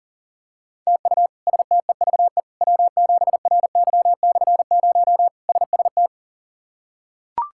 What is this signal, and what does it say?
Morse code, heard as a keyed tone, about 40 wpm, 700 Hz. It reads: TU STEVE W7RY/0 SST
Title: K1USN SST QSO Practice - 40wpm